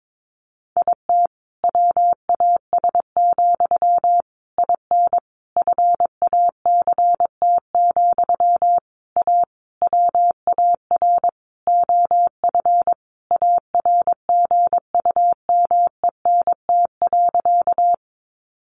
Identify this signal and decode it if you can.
Morse code, heard as a keyed tone, about 22 wpm, 700 Hz. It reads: IT WAS, IN FACT, A WAR OF ARGUMENT.
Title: Sentences from Top 1400 Words 22wpm